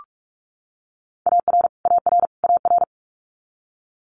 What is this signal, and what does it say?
Morse code, heard as a keyed tone, about 46 wpm, 690 Hz. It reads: AR AR AR
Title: Multi-letter Phonemes Repeated 3x 45wpm